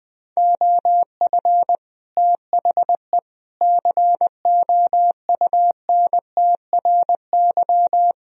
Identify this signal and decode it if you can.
Morse code, heard as a keyed tone, about 20 wpm, 700 Hz. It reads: OF THE COUNTRY
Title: Speed-Racing - Sets of 3 Words 17wpm